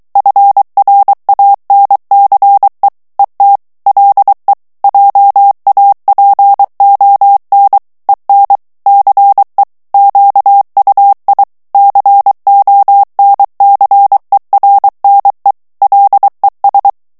Morse code, heard as a keyed tone, about 23 wpm, 800 Hz. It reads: FRANCE ET LE JAPON EN CE QUI CONCERNE LES
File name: FRANCAIS TEXTE CLAIR 1 VITESSE 22 MOTS MINUTES